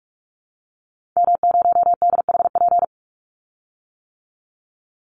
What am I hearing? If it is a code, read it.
Morse code, heard as a keyed tone, about 44 wpm, 700 Hz. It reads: M0DHP
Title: Call Signs - 45wpm